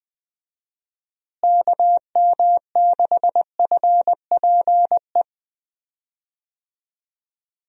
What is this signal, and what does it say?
Morse code, heard as a keyed tone, about 20 wpm, 700 Hz. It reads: KM6FPE
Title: USA Call Signs - 6 Characters - Repeated 2x - 20wpm